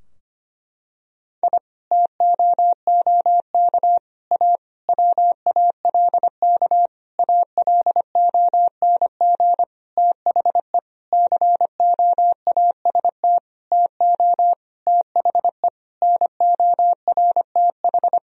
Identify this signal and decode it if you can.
Morse code, heard as a keyed tone, about 25 wpm, 700 Hz. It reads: I TOOK A WALK ALONG THE COAST TO THE NORTH
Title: Sentences from Top 1000 Words - Review Part 4 - 25wpm